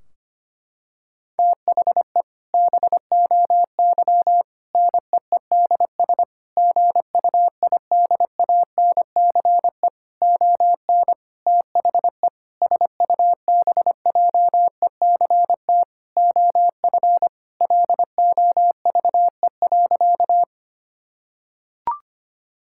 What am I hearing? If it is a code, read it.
Morse code, heard as a keyed tone, about 25 wpm, 700 Hz. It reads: THE BOY NEEDS GUIDANCE ON THE SUBJECT OF LOVE.
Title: Sentences from Top 1600 Words 25wpm